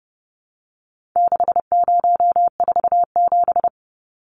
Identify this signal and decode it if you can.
Morse code, heard as a keyed tone, about 30 wpm, 700 Hz. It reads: 6047
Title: Numbers - 4-Digit 30wpm